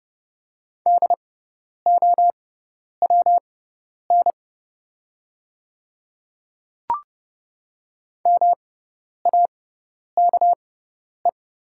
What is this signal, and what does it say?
Morse code, heard as a keyed tone, about 30 wpm, spaced as for 10 wpm, 700 Hz. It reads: DOWN MAKE
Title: 6x Character Spacing - Top 100 Words - 30wpm